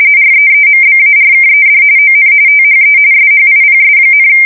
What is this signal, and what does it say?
RTTY, about 45.45 baud. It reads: REEMBARKED ON THE SHIPS,
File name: fsk170 / usnavy2